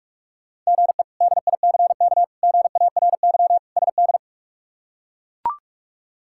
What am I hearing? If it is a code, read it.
Morse code, heard as a keyed tone, about 41 wpm, 700 Hz. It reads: GE DICK GARY SD
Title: K1USN SST QSO Practice - 45wpm